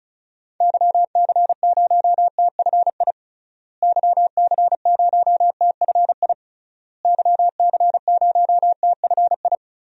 Morse code, heard as a keyed tone, about 34 wpm, 700 Hz. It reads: YC0TFI YC0TFI YC0TFI
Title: DXCC-100 Call Signs - 6 Characters - Repeated 3x - 35wpm